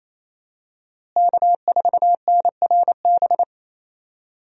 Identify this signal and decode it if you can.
Morse code, heard as a keyed tone, about 28 wpm, 700 Hz. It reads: K4NRB
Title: USA Call Signs - 5 Characters - Repeated 3x - 28wpm